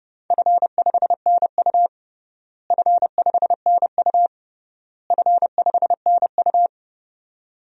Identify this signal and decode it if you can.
Morse code, heard as a keyed tone, about 30 wpm, 700 Hz. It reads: F5NU F5NU F5NU
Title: DXCC-100 Call Signs - 4 Characters - Repeated 3x - 30wpm